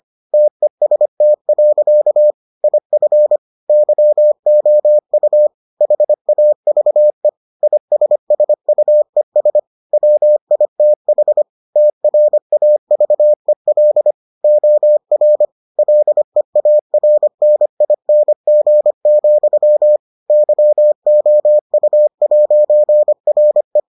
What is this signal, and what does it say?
Morse code, heard as a keyed tone, about 25 wpm, 600 Hz. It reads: TEST. IF YOU HAVE ISSUES WITH TRAVEL OR LEARNING, YOU'RE